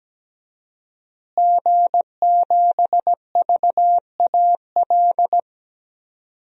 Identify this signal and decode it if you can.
Morse code, heard as a keyed tone, about 17 wpm, 700 Hz. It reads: G7VAL
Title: DXCC-100 Call Signs - 5 Characters - Repeated 2x - 17wpm